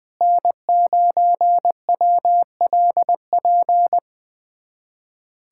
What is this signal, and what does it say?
Morse code, heard as a keyed tone, about 20 wpm, 700 Hz. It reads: N9WLP
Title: USA Callsign DE Callsign - 5 Characters - 20wpm